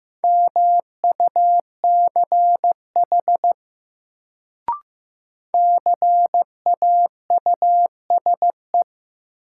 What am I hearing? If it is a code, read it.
Morse code, heard as a keyed tone, about 15 wpm, 700 Hz. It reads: MUCH CAUSE
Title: Sentences from Top 200 Words - New Word Drill - 15wpm